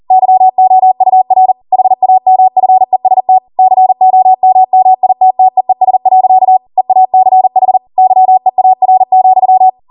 Morse code, heard as a keyed tone, about 40 wpm, 750 Hz. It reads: YOUR HAMFEST COMMITTEES. EACH YEAR,